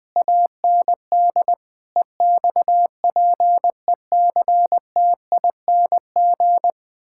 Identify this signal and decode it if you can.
Morse code, heard as a keyed tone, about 21 wpm, 700 Hz. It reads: AND EXPECTING
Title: Sentences with 8-Word Phrases - 20wpm